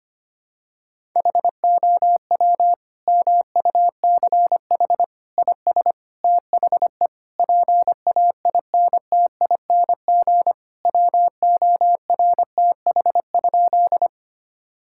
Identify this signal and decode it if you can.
Morse code, heard as a keyed tone, about 25 wpm, 700 Hz. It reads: HOW MUCH IS THE PAINTING WORTH?
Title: Sentences from Top 1000 Words - Review Part 6 - 25wpm